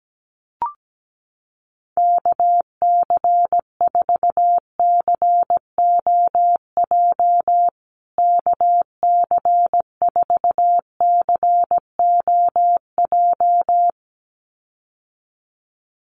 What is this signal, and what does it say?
Morse code, heard as a keyed tone, about 17 wpm, 700 Hz. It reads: KC4COJ KC4COJ
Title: USA Call Signs Repeated 2x 17wpm